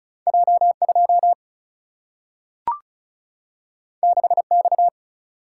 Morse code, heard as a keyed tone, about 35 wpm, 700 Hz. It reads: J2 6X
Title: Single-Letter Single-Number 35wpm